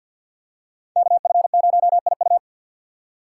Morse code, heard as a keyed tone, about 53 wpm, 700 Hz. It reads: KF0IU
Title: USA Call Signs - 5 Characters - 6x Character Spacing - 50wpm